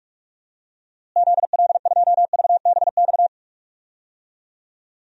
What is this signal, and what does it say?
Morse code, heard as a keyed tone, about 41 wpm, 700 Hz. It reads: ZL2VBX
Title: DXCC-100 Call Signs - 6 Characters - Repeated 2x - 45wpm